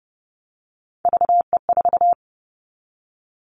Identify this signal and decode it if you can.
Morse code, heard as a keyed tone, about 30 wpm, 700 Hz. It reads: VE4
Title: Callsign Prefixes for Top 100 DXCC - 30wpm